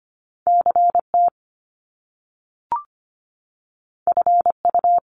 Morse code, heard as a keyed tone, about 25 wpm, 700 Hz. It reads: CT FU
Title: Top Two Letter Combinations in Words 25wpm